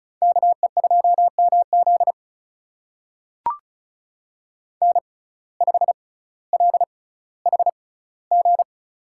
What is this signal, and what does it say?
Morse code, heard as a keyed tone, about 34 wpm, spaced as for 18 wpm, 700 Hz. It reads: KE2MZ N5LHG
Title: USA Call Signs - 5 Characters - 6x Character Spacing - 35wpm